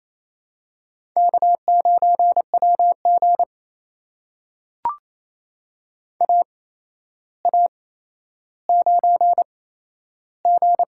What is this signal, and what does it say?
Morse code, heard as a keyed tone, about 27 wpm, spaced as for 12 wpm, 700 Hz. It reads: K9WG AA9G
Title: USA Call Signs - 4 Characters - 8x Character Spacing - 28wpm